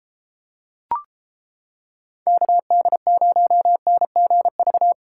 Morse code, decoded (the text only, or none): KD0NGV